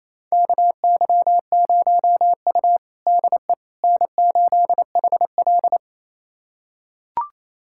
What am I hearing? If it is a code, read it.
Morse code, heard as a keyed tone, about 28 wpm, 700 Hz. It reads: KY0U DE N8HL
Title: USA Callsign DE Callsign - 4 Characters - 28wpm